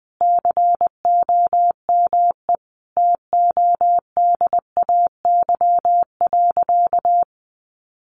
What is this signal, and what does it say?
Morse code, heard as a keyed tone, about 20 wpm, 700 Hz. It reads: COME TODAY.
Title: Easy Phrases 20wpm